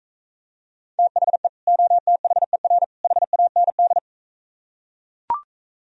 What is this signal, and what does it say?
Morse code, heard as a keyed tone, about 43 wpm, 700 Hz. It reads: THE OTHER HAND